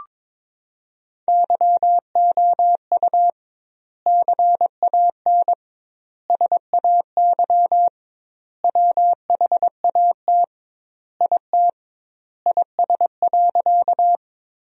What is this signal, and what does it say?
Morse code, heard as a keyed tone, about 22 wpm, 700 Hz. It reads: YOU CAN SAY WHAT IT IS.
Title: 2x Word-Spacing - Sentences from Top 200 Words - 22wpm